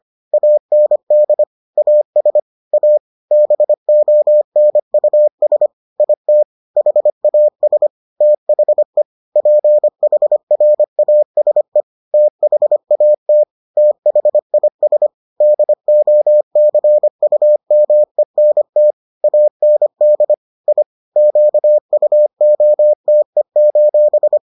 Morse code, heard as a keyed tone, about 25 wpm, 600 Hz. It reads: AND AS A BONUS IT HAS THE PHRASE THAT THIS DOCUMENT AND I QUOTE: